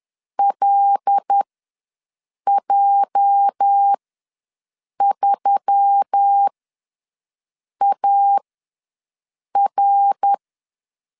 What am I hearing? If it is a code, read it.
Morse code, heard as a keyed tone, about 10 wpm, spaced as for 6 wpm, 800 Hz. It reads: LJ3AR
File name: LEZIONI / 8 YJ45 / 8 YJ ws 10wpm 4043